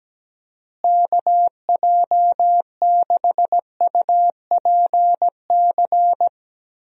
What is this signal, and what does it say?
Morse code, heard as a keyed tone, about 17 wpm, 700 Hz. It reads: KJ6UPC
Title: USA Call Signs - 6 Characters - Repeated 2x - 17wpm